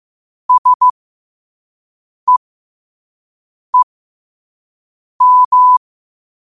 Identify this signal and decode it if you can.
Morse code, heard as a keyed tone, about 15 wpm, spaced as for 5 wpm, 1000 Hz. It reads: SEEM
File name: 5wpm